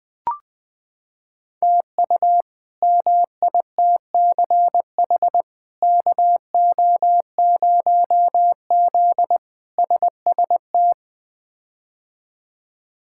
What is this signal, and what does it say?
Morse code, heard as a keyed tone, about 21 wpm, 700 Hz. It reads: TU MITCH KO0Z SST
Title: K1USN SST QSO Practice - 20wpm